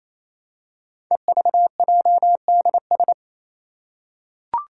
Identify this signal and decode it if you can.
Morse code, heard as a keyed tone, about 28 wpm, 700 Hz. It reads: EVJDS
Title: Code Groups - 5 Letters - 28wpm